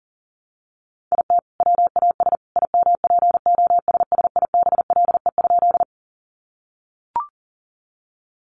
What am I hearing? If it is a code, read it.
Morse code, heard as a keyed tone, about 40 wpm, 700 Hz. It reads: IT WAS IMPOSSIBLE?
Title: Sentences from Top 1100 Words 40wpm